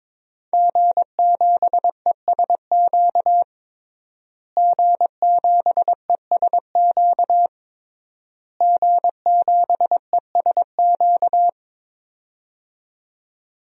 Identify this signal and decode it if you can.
Morse code, heard as a keyed tone, about 22 wpm, 700 Hz. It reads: G7ESQ G7ESQ G7ESQ
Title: DXCC-100 Call Signs - 5 Characters - Repeated 3x - 22wpm